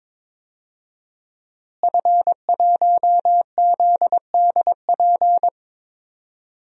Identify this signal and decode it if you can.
Morse code, heard as a keyed tone, about 21 wpm, 700 Hz. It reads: F1ZDP
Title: DXCC-100 Call Signs - 5 Characters - Repeated 3x - 22wpm